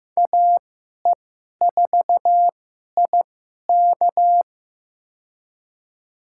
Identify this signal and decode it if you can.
Morse code, heard as a keyed tone, about 15 wpm, spaced as for 11 wpm, 700 Hz. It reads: AE4IK
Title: USA Call Signs - 5 Characters - 2x Character Spacing - 15wpm